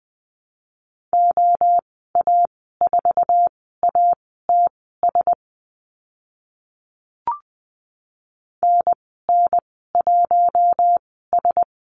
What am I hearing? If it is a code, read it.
Morse code, heard as a keyed tone, about 20 wpm, spaced as for 14 wpm, 700 Hz. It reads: OA4ATS NN1S
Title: 076 Lesson 13g - Letter L - Call Signs 20wpm